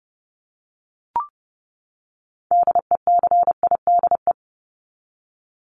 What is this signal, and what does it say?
Morse code, heard as a keyed tone, about 30 wpm, 700 Hz. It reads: DECIDE
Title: ED Suffix - Top 100 Words with Base Words 30wpm